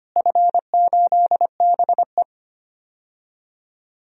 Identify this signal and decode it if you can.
Morse code, heard as a keyed tone, about 25 wpm, 700 Hz. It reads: F8BE
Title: DXCC-100 Call Signs - 4 Characters - Repeated 3x - 25wpm